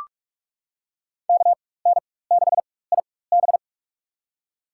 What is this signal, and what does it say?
Morse code, heard as a keyed tone, about 44 wpm, spaced as for 21 wpm, 700 Hz. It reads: KN6IB
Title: USA Call Signs - 5 Characters - 4x Character Spacing - 45wpm